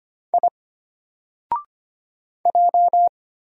Morse code, heard as a keyed tone, about 25 wpm, 700 Hz. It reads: I J
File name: Single-Letter-Number-25wpm